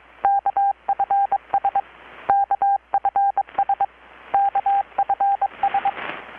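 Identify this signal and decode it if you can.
Morse code, heard as a keyed tone, about 23 wpm, 800 Hz, with radio noise. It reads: KFS KFS KFS
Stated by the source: Morse from KPH's Night of Nights XX maritime radiotelegraphy event, received in Chiapas, Mexico about 3,800 km away on a Kenwood R-600 with a grounded 10 m long-wire antenna.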